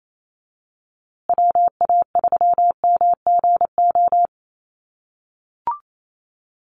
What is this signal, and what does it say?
Morse code, heard as a keyed tone, about 28 wpm, 700 Hz. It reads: WA3MGO
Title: USA Call Signs Repeated 3x 28wpm